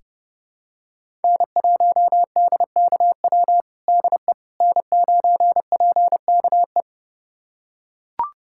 Morse code, heard as a keyed tone, about 30 wpm, 700 Hz. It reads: N1DKW DE N9PKE
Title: USA Callsign DE Callsign - 5 Characters - 30wpm